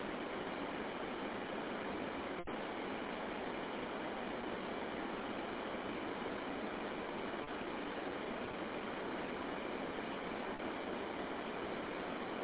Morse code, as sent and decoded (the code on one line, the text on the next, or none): none